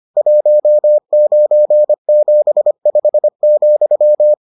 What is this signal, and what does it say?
Morse code, heard as a keyed tone, about 25 wpm, 600 Hz. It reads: 1975,